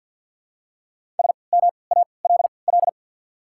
Morse code, heard as a keyed tone, about 48 wpm, spaced as for 28 wpm, 700 Hz. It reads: SMALL
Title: 3x Character Spacing - Top 200 Words - 50wpm